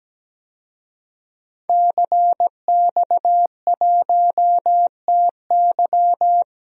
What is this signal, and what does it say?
Morse code, heard as a keyed tone, about 17 wpm, 700 Hz. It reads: CX1TY
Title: DXCC-100 Call Signs - 5 Characters - 17wpm